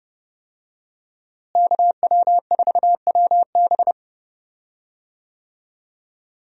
Morse code, heard as a keyed tone, about 30 wpm, 700 Hz. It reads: KW4WB
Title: USA Call Signs - 5 Characters - Repeated 2x - 30wpm